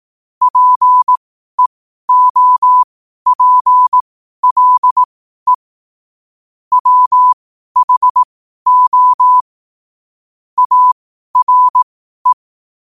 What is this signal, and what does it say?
Morse code, heard as a keyed tone, about 18 wpm, spaced as for 12 wpm, 1000 Hz. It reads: PEOPLE WHO ARE
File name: qotdfiles / 12wpm